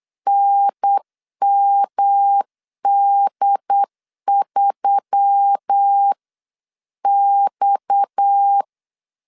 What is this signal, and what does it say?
Morse code, heard as a keyed tone, about 8 wpm, 800 Hz. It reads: NMD3X
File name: 8wpm